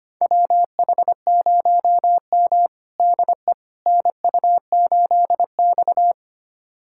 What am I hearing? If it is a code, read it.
Morse code, heard as a keyed tone, about 25 wpm, 700 Hz. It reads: WH0M DE NU8X
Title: USA Callsign DE Callsign - 4 Characters - 25wpm